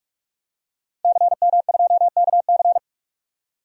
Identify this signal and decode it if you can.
Morse code, heard as a keyed tone, about 46 wpm, 700 Hz. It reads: CM2KC
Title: DXCC-100 Call Signs - 5 Characters - 45wpm